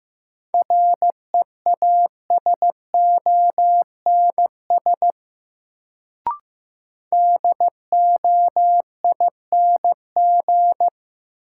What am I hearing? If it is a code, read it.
Morse code, heard as a keyed tone, about 15 wpm, 700 Hz. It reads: REASONS DOING
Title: Sets of 4 Words - New Word Drill - 15wpm